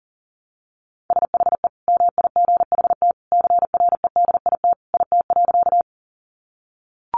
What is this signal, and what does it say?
Morse code, heard as a keyed tone, about 40 wpm, 700 Hz. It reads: SHE MIGHT CREDIT IT.
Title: Sentences from Top 1000 Words 40wpm